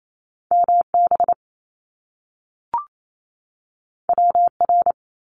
Morse code, heard as a keyed tone, about 28 wpm, 700 Hz. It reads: MB WR
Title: Top Two Letter Combinations in Words 28wpm